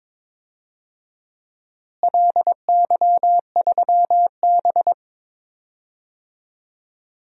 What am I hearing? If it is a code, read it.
Morse code, heard as a keyed tone, about 22 wpm, 700 Hz. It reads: LY3B
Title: DXCC-100 Call Signs - 4 Characters - Repeated 3x - 22wpm